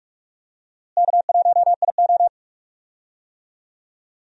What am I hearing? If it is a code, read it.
Morse code, heard as a keyed tone, about 46 wpm, 700 Hz. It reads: K1IO
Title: USA Call Signs - 4 Characters - Repeated 2x - 45wpm